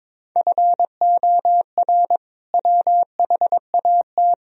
Morse code, heard as a keyed tone, about 22 wpm, 700 Hz. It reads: FOR WHAT
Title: Sentences from Top 1000 Words - Review Part 4 - 22wpm